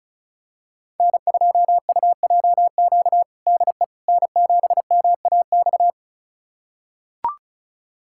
Morse code, heard as a keyed tone, about 34 wpm, 700 Hz. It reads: N2UJQ DE N7MAX
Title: USA Callsign DE Callsign - 5 Characters - 35wpm